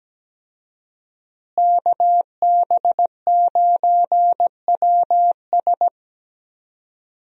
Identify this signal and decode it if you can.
Morse code, heard as a keyed tone, about 17 wpm, 700 Hz. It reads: KB9WS
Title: USA Call Signs - 5 Characters - Repeated 2x - 17wpm